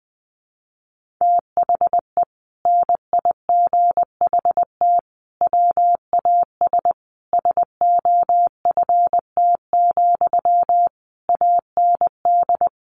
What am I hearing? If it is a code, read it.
Morse code, heard as a keyed tone, about 20 wpm, 700 Hz. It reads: THE NIGHT WAS SOFT, AND
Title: Sentences from Top 1500 Words 20wpm